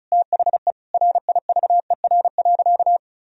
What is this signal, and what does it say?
Morse code, heard as a keyed tone, about 36 wpm, 700 Hz. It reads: THE RIVER.
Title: Sentences from Top 1000 Words - Review Part 4 - 35wpm